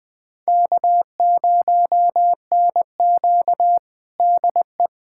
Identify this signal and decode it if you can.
Morse code, heard as a keyed tone, about 20 wpm, 700 Hz. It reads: K0NQ DE
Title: USA Callsign DE Callsign - 4 Characters - 20wpm